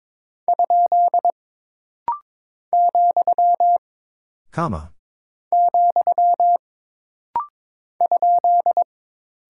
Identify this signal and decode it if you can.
Morse code, heard as a keyed tone, about 22 wpm, 700 Hz. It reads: ? , , ?
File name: practice / 22wpm / Single Punctuation-Prosign - Rapid-Fire - 22wpm